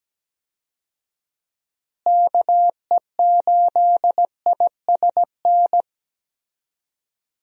Morse code, heard as a keyed tone, about 17 wpm, 700 Hz. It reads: KE8ISN
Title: USA Call Signs - 6 Characters - Repeated 2x - 17wpm